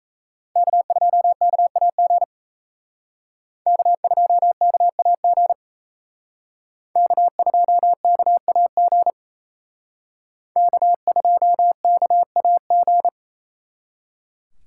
Morse code, heard as a keyed tone, about 36 wpm, 700 Hz. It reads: K2KAG K2KAG K2KAG K2KAG